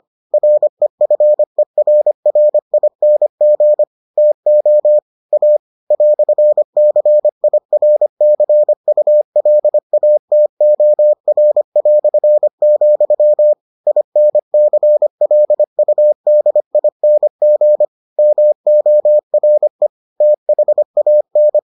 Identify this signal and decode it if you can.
Morse code, heard as a keyed tone, about 25 wpm, 600 Hz. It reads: REFERRING TO A "CIRCULATOR", INCLUDING MORE THAN